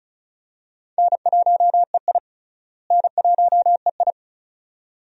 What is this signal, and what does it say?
Morse code, heard as a keyed tone, about 36 wpm, 700 Hz. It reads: N1EI N1EI
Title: USA Call Signs - 4 Characters - Repeated 2x - 35wpm